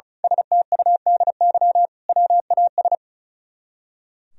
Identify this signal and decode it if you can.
Morse code, heard as a keyed tone, about 34 wpm, 700 Hz. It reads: STUDY WAS
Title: Speed-Racing - Sets of 5 Words 35wpm